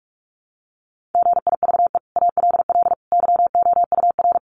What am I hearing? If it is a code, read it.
Morse code, heard as a keyed tone, about 44 wpm, 700 Hz. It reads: GIVE ALL YOUR
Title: Sentences from Top 600 Words 45wpm